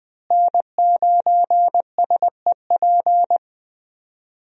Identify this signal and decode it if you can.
Morse code, heard as a keyed tone, about 20 wpm, 700 Hz. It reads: N9SEP
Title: USA Call Signs - 5 Characters - Repeated 2x - 20wpm